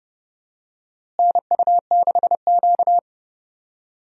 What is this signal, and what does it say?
Morse code, heard as a keyed tone, about 31 wpm, 700 Hz. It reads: NU6Q
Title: USA Call Signs - 4 Characters - Repeated 3x - 30wpm